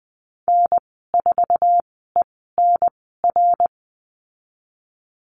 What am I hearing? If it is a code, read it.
Morse code, heard as a keyed tone, about 20 wpm, spaced as for 15 wpm, 700 Hz. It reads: N4ENR
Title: 055 Lesson 10g - Letter r - Call Signs 20wpm